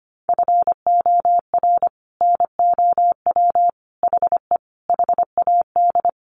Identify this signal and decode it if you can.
Morse code, heard as a keyed tone, about 25 wpm, 700 Hz. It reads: FOR NOW HE HAD
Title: Sentences from Top 1100 Words 25wpm